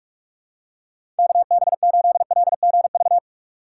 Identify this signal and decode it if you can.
Morse code, heard as a keyed tone, about 46 wpm, 700 Hz. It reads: KB8LGV
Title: USA Call Signs - 6 Characters - 3x Character Spacing - 45wpm